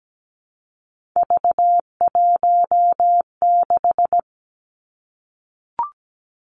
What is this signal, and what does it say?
Morse code, heard as a keyed tone, about 17 wpm, 700 Hz. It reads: V16
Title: Three Challenging Letter-Number Combinations 17wpm